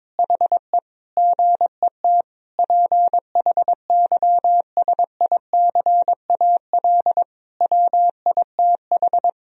Morse code, heard as a keyed tone, about 22 wpm, 700 Hz. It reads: HE GET PHYSICAL WITH
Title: Sentences from Top 1000 Words - Review Part 4 - 22wpm